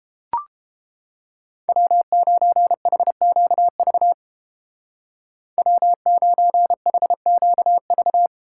Morse code, heard as a keyed tone, about 29 wpm, 700 Hz. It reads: W9HQV W9HQV